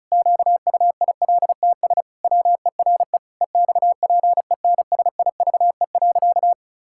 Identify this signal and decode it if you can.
Morse code, heard as a keyed tone, about 34 wpm, 700 Hz. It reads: QUILTS WERE EXPENSIVE.
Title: Pangram Challenge v2 - 35wpm